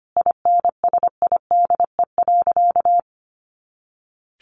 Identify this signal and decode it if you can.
Morse code, heard as a keyed tone, about 25 wpm, 700 Hz. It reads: INSIDE.
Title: Easy Phrases 25wpm